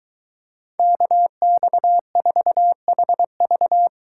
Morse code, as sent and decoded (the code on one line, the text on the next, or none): -.- -..- ....- .... ...-
KX4HV